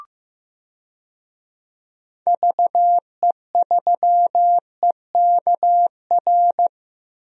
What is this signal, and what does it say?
Morse code, heard as a keyed tone, about 15 wpm, 700 Hz. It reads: VE3EKR
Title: DXCC-100 Call Signs - 6 Characters - Repeated 3x - 15wpm